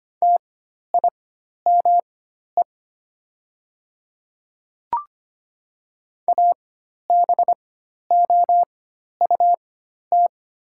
Morse code, heard as a keyed tone, about 25 wpm, spaced as for 12 wpm, 700 Hz. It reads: TIME ABOUT
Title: 4x Character Spacing - Top 100 Words - 25wpm